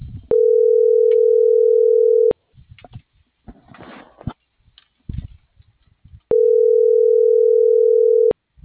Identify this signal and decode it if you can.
Ringback tone (North America), 440 + 480 Hz.